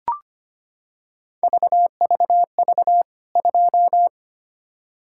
Morse code, heard as a keyed tone, about 25 wpm, 700 Hz. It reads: VVV 2
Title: K1USN SST QSO Practice 25wpm